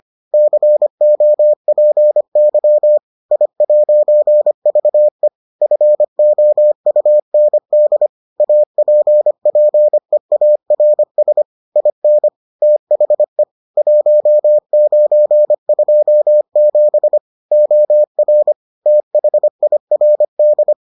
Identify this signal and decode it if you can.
Morse code, heard as a keyed tone, about 25 wpm, 600 Hz. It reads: COPY I'VE FOUND APPEARS IN THE 1927 OR THIRD